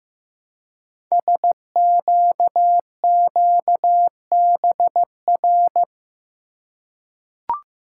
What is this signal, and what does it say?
Morse code, heard as a keyed tone, about 15 wpm, 700 Hz. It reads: SQQBR
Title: Code Groups - 5 Letters - 15wpm